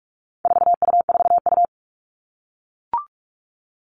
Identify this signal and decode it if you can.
Morse code, heard as a keyed tone, about 44 wpm, 700 Hz. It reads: 4U4U
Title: All Cut Numbers - 4-Digit 45wpm